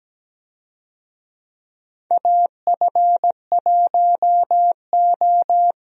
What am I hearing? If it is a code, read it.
Morse code, heard as a keyed tone, about 17 wpm, 700 Hz. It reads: AF1O
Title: USA Call Signs - 4 Characters - Repeated 2x - 17wpm